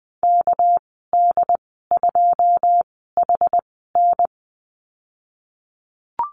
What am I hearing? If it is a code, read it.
Morse code, heard as a keyed tone, about 20 wpm, spaced as for 14 wpm, 700 Hz. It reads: KD2HN
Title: 213 Lesson 34g - Number 8 - Call Signs 20wpm